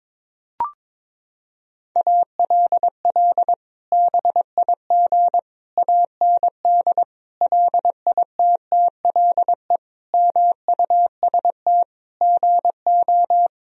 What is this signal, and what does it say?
Morse code, heard as a keyed tone, about 22 wpm, 700 Hz. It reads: ALL BIG AND LITTLE MUST GO
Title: Sentences from Top 1000 Words - Review Part 1 - 22wpm